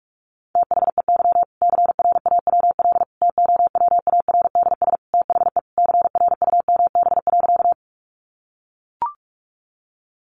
Morse code, heard as a keyed tone, about 44 wpm, 700 Hz. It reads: THEY CRAWL TOWARDS THE CRUMB.
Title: Easy Phrases 45wpm